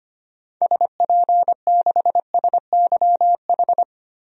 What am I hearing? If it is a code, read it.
Morse code, heard as a keyed tone, about 25 wpm, 700 Hz. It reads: SP6SYH
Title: DXCC-100 Call Signs - 6 Characters - 25wpm